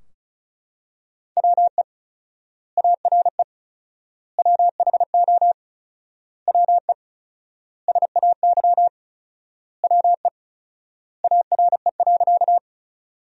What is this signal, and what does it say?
Morse code, heard as a keyed tone, about 34 wpm, 700 Hz. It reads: WE ARE WHO WE SAY WE ARE.
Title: 4x Word-Spacing - Sentences from Top 200 Words - 35wpm